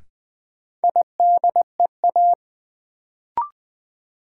Morse code, heard as a keyed tone, about 20 wpm, 700 Hz. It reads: IDEA
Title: Sets of 3 Words - New Word Drill - 20wpm